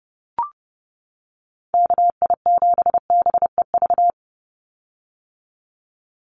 Morse code, heard as a keyed tone, about 30 wpm, 700 Hz. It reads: KI7BEV
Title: USA Call Signs 30wpm